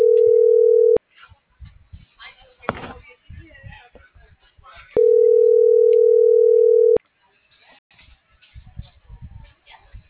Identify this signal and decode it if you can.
Ringback tone (North America), 440 + 480 Hz.